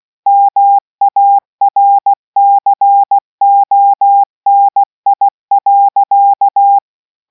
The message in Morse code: MARCONI.